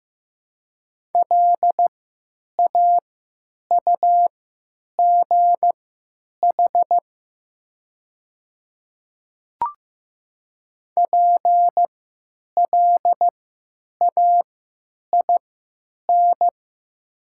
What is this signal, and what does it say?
Morse code, heard as a keyed tone, about 15 wpm, spaced as for 9 wpm, 700 Hz. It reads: LAUGH PLAIN
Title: 3x Character Spacing - Top 500 Words - 15wpm